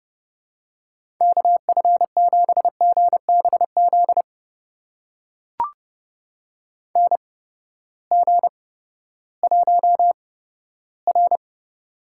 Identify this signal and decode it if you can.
Morse code, heard as a keyed tone, about 30 wpm, spaced as for 15 wpm, 700 Hz. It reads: KF7GBZ NG1R
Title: USA Call Signs - 8x Character Spacing - 30wpm